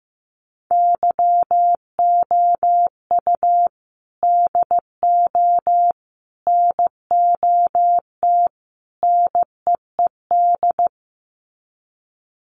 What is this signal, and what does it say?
Morse code, heard as a keyed tone, about 15 wpm, 700 Hz. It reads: YOU DO NOT NEED
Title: Sets of 4 Words 15wpm